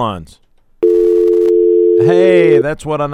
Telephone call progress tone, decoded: dial tone